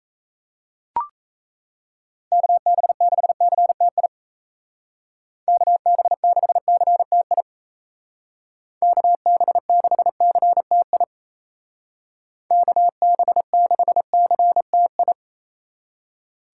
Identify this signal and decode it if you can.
Morse code, heard as a keyed tone, about 37 wpm, 700 Hz. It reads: KB6CTI KB6CTI KB6CTI KB6CTI